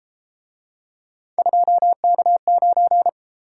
Morse code, heard as a keyed tone, about 28 wpm, 700 Hz. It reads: 2K9
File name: Speed-Racing - Three Letter-Number Combinations - 22wpm